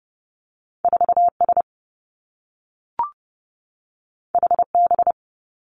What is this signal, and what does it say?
Morse code, heard as a keyed tone, about 30 wpm, 700 Hz. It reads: 4 S H B
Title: Two Challenging Letter-Number Combinations 30wpm